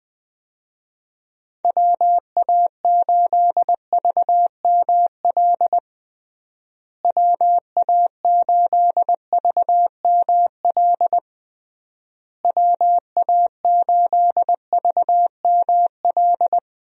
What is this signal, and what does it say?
Morse code, heard as a keyed tone, about 20 wpm, 700 Hz. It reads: WA8VML WA8VML WA8VML
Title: USA Call Signs - 6 Characters - Repeated 3x - 20wpm